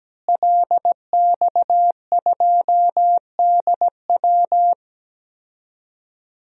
Morse code, heard as a keyed tone, about 17 wpm, 700 Hz. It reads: LX2DW
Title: DXCC-100 Call Signs - 5 Characters - Repeated 2x - 17wpm